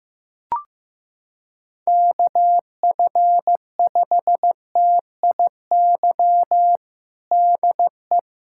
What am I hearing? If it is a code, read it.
Morse code, heard as a keyed tone, about 15 wpm, 700 Hz. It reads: KF5TIY DE
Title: USA Callsign DE Callsign - 6 Characters - 15wpm